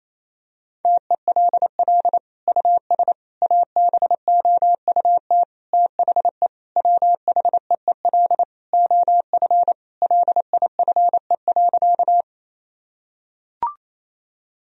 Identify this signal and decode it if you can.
Morse code, heard as a keyed tone, about 28 wpm, 700 Hz. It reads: TELL US ABOUT THE WHEEL OF LIFE.
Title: Sentences from Top 1000 Words - Review Part 3 - 28wpm